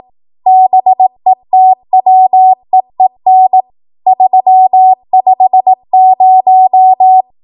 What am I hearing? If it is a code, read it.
Morse code, heard as a keyed tone, about 18 wpm, 750 Hz. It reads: BETWEEN 350
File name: Morse / 18WPM